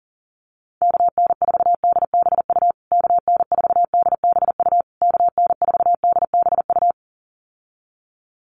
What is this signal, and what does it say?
Morse code, heard as a keyed tone, about 40 wpm, 700 Hz. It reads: KN4DBU KN4DBU KN4DBU
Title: USA Call Signs Repeated 3x 40wpm